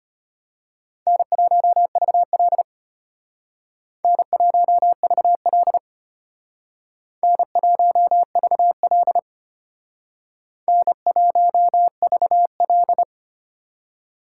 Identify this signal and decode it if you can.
Morse code, heard as a keyed tone, about 33 wpm, 700 Hz. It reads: N1VL N1VL N1VL N1VL